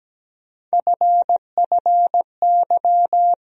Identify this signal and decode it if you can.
Morse code, heard as a keyed tone, about 17 wpm, 700 Hz. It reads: FFY
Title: USA Call Sign Postfixes - 3 Letters 17wpm